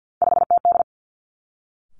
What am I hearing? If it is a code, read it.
Morse code, heard as a keyed tone, about 50 wpm, 700 Hz. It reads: 5TD
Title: Three Challenging Letter-Number Combinations 50wpm